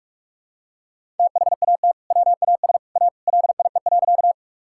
Morse code, heard as a keyed tone, about 41 wpm, 700 Hz. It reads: THAT WAS A LIE.
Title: Sentences from Top 1600 Words 45wpm